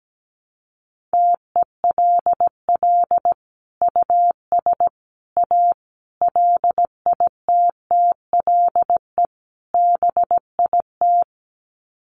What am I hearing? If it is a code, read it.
Morse code, heard as a keyed tone, about 17 wpm, 700 Hz. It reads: TELL US A LITTLE BIT
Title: Sets of 5 Words Encore2 17wpm